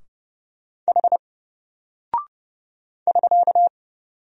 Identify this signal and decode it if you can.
Morse code, heard as a keyed tone, about 30 wpm, 700 Hz. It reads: H <SK>
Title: Single Character (Letter:Number:Punctuation:Prosign) Alt 30wpm